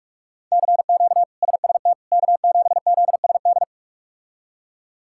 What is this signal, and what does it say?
Morse code, heard as a keyed tone, about 43 wpm, 700 Hz. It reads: CQ SST K7ZSD
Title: K1USN SST QSO Practice - 45wpm